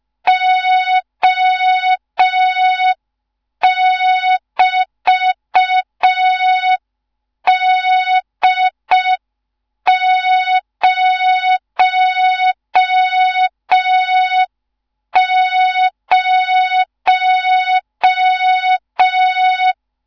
Morse code, heard as a keyed tone, about 5 wpm, 3700 Hz. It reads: O<BT>D00
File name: Morze-HA2MN / hang / 15 GYAKORLAT